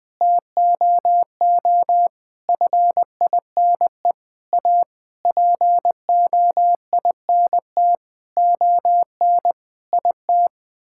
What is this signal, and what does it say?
Morse code, heard as a keyed tone, about 20 wpm, 700 Hz. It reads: TOO FINE A POINT ON IT
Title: Sets of 9 Words - 20wpm